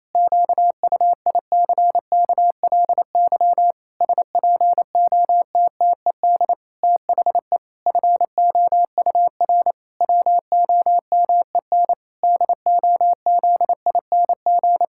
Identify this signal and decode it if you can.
Morse code, heard as a keyed tone, about 28 wpm, 700 Hz. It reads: QUICKLY SPOTTED THE FOUR WOMEN DOZING
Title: Pangram Challenge v2 - 28wpm